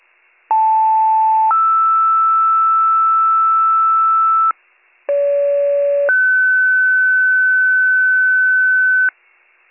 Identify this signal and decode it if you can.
Two-tone page, heard as a one-second tone, then a three-second tone, about 880 then 1320 Hz, about 570 then 1500 Hz, with noise. It is heard from 0.5 to 9.1 s.